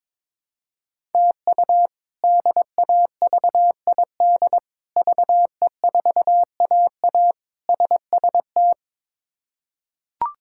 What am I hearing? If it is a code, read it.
Morse code, heard as a keyed tone, about 23 wpm, 700 Hz. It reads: TU DAVID VE4AA SST
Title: K1USN SST QSO Practice - 22wpm